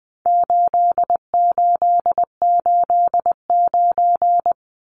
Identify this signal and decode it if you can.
Morse code, heard as a keyed tone, about 20 wpm, 700 Hz. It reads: 8889
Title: Numbers - 4-Digit 20wpm